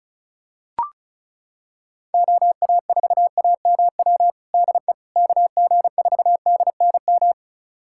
Morse code, heard as a keyed tone, about 36 wpm, 700 Hz. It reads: OA4AMW DE KG4DNM
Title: USA Callsign DE Callsign 35wpm